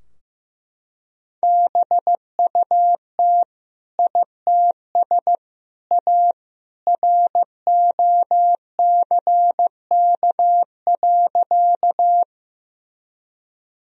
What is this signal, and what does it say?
Morse code, heard as a keyed tone, about 16 wpm, 700 Hz. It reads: BUT ITS A ROCK.
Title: Sentences from Top 1000 Words - Review Part 2 - 15wpm